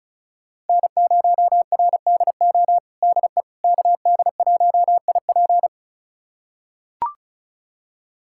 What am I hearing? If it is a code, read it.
Morse code, heard as a keyed tone, about 34 wpm, 700 Hz. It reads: N0RDO DE KD1IP
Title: USA Callsign DE Callsign - 5 Characters - 35wpm